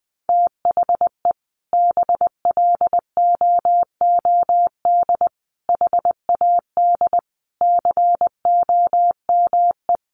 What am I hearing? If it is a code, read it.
Morse code, heard as a keyed tone, about 20 wpm, 700 Hz. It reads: THE BLOOD HAD COME
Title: Sentences from Top 1300 Words 20wpm